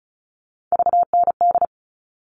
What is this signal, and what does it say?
Morse code, heard as a keyed tone, about 34 wpm, 700 Hz. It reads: VND
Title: All Cut Numbers - 3-Digit 35wpm